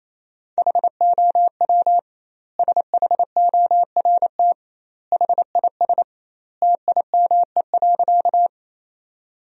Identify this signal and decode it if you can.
Morse code, heard as a keyed tone, about 28 wpm, 700 Hz. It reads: HOW SHORT HIS TIME.
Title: 2x Word-Spacing - Sentences from Top 400 Words - 28wpm